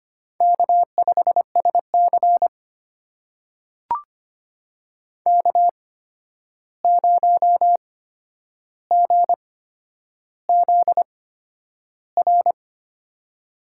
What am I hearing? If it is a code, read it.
Morse code, heard as a keyed tone, about 25 wpm, spaced as for 10 wpm, 700 Hz. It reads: K5SC K0GZR
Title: USA Call Signs - 8x Character Spacing - 25wpm